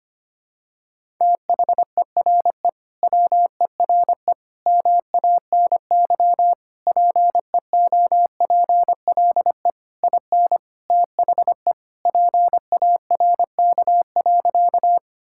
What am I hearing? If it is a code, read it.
Morse code, heard as a keyed tone, about 25 wpm, 700 Hz. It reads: THERE WERE MANY PEOPLE IN THE PARK.
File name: practice / 25wpm / Sentences with 3-Word Phrases - 25wpm